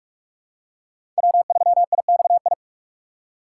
Morse code, heard as a keyed tone, about 46 wpm, 700 Hz. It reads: W3IXI
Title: USA Call Signs - 3x Character Spacing - 45wpm